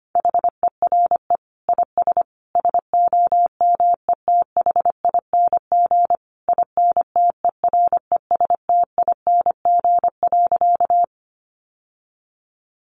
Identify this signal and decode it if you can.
Morse code, heard as a keyed tone, about 25 wpm, 700 Hz. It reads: HERE IS SOMETHING INTERESTING.
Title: Sentences from Top 1000 Words 25wpm